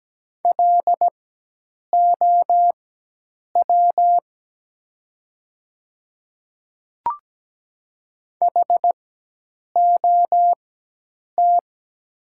Morse code, heard as a keyed tone, about 17 wpm, spaced as for 8 wpm, 700 Hz. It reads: LOW HOT
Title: 4x Character Spacing - Top 500 Words - 17wpm